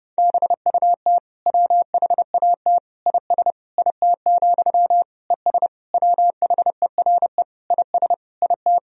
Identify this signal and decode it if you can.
Morse code, heard as a keyed tone, about 30 wpm, 700 Hz. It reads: BUT WHAT IS IT, ES WHERE IS IT
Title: Sentences from Top 300 Words and Abbreviations 30wpm